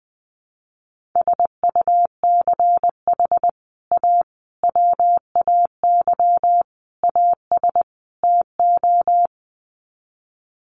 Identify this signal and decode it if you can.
Morse code, heard as a keyed tone, about 20 wpm, 700 Hz. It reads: SUCH A WAY AS TO
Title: Sets of 5 Words Encore2 20wpm